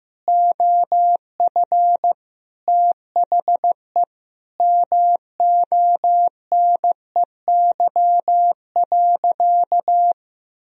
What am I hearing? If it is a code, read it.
Morse code, heard as a keyed tone, about 15 wpm, 700 Hz. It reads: OF THE MONEY.
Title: Sentences from Top 1000 Words - Review Part 2 - 15wpm